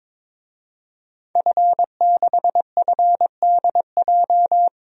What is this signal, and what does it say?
Morse code, heard as a keyed tone, about 22 wpm, 700 Hz. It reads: F6FDJ
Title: DXCC-100 Call Signs - 5 Characters - 22wpm